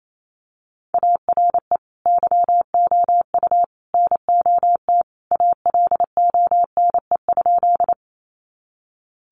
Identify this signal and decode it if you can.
Morse code, heard as a keyed tone, about 28 wpm, 700 Hz. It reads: ARE YOU NOT ALONE?
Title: Sentences from Top 1000 Words 28wpm